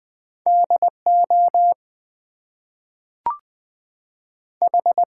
Morse code, heard as a keyed tone, about 20 wpm, 700 Hz. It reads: DO H
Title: Two Letter Words - 20wpm